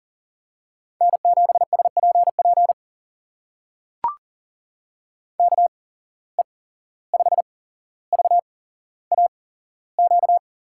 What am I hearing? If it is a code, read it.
Morse code, heard as a keyed tone, about 38 wpm, spaced as for 16 wpm, 690 Hz. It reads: N7SPP KE5VAQ
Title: USA Call Signs - 8x Character Spacing - 40wpm